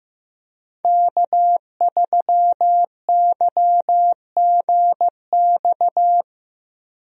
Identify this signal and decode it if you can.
Morse code, heard as a keyed tone, about 15 wpm, 700 Hz. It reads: K3YGX
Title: USA Call Signs - 5 Characters - Repeated 2x - 15wpm